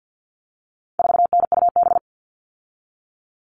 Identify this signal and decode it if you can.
Morse code, heard as a keyed tone, about 50 wpm, 700 Hz. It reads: 4NUB
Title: All Cut Numbers - 4-Digit 50wpm